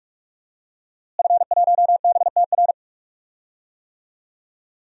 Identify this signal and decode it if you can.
Morse code, heard as a keyed tone, about 47 wpm, 700 Hz. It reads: F1BTR
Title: DXCC-100 Call Signs - 5 Characters - Repeated 3x - 45wpm